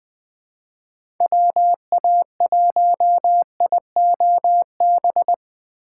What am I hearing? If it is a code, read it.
Morse code, heard as a keyed tone, about 21 wpm, 700 Hz. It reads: WA1IOB
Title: USA Call Signs - 6 Characters - 4x Character Spacing - 20wpm